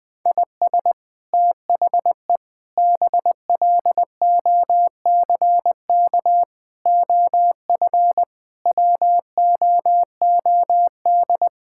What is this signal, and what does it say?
Morse code, heard as a keyed tone, about 20 wpm, 700 Hz. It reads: IS THE BLOCK OF WOOD
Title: Sentences from Top 1600 Words 20wpm